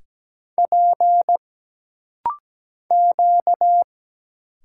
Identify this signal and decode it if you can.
Morse code, heard as a keyed tone, about 17 wpm, 700 Hz. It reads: P Q
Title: Single Letters - Rapid-Fire 17wpm